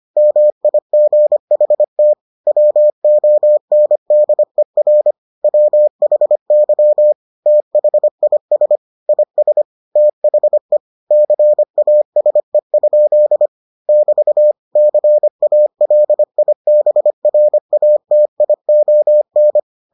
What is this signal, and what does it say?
Morse code, heard as a keyed tone, about 25 wpm, 600 Hz. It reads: MIGHT WONDER WHY THIS IS THE CASE? <BT> CALIBRATION